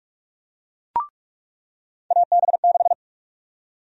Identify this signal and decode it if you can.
Morse code, heard as a keyed tone, about 46 wpm, 700 Hz. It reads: AB6
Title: USA Call Sign Prefixes - 45wpm